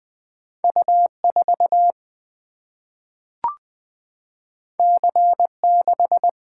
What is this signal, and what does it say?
Morse code, heard as a keyed tone, about 20 wpm, 700 Hz. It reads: U 4 C 6
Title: Single-Letter Single-Number 20wpm